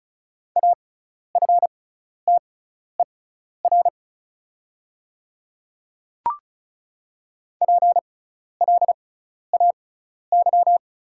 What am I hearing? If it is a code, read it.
Morse code, heard as a keyed tone, about 36 wpm, spaced as for 12 wpm, 690 Hz. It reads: AFTER PLAY
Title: 6x Character Spacing - Top 300 Words - 35wpm